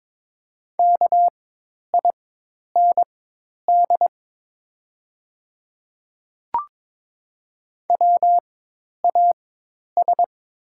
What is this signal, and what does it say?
Morse code, heard as a keyed tone, about 22 wpm, spaced as for 10 wpm, 690 Hz. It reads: KIND WAS
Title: 4x Character Spacing - Top 300 Words - 22wpm